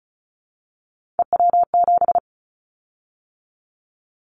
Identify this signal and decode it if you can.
Morse code, heard as a keyed tone, about 34 wpm, 700 Hz. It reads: EW7
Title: Callsign Prefixes for Top 50 DXCC - 35wpm